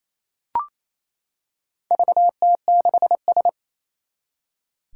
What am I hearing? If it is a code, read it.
Morse code, heard as a keyed tone, about 27 wpm, 700 Hz. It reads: VT6S
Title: Four Challenging Letter-Number Combinations 28wpm